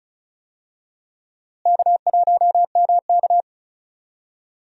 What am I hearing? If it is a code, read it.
Morse code, heard as a keyed tone, about 34 wpm, 700 Hz. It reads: K1MK
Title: USA Call Signs - 4 Characters - Repeated 3x - 35wpm